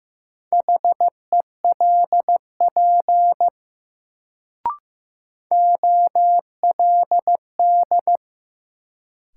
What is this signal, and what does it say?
Morse code, heard as a keyed tone, about 15 wpm, 700 Hz. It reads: HELP OLD
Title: Sets of 2 Words - New Word Drill - 15wpm